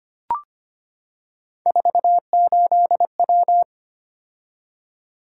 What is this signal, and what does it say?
Morse code, heard as a keyed tone, about 25 wpm, 700 Hz. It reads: 48W
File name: Three Letter-Number Combinations - 25wpm